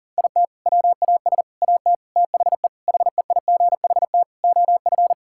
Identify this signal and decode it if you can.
Morse code, heard as a keyed tone, about 40 wpm, 700 Hz. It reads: IT WAS AT THE HEIGHT OF
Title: Sentences from Top 1600 Words 40wpm